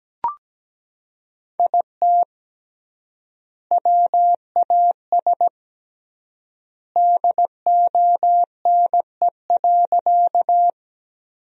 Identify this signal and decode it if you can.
Morse code, heard as a keyed tone, about 17 wpm, 700 Hz. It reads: IT WAS DONE.
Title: 3x Word-Spacing - Sentences from Top 500 Words - 17wpm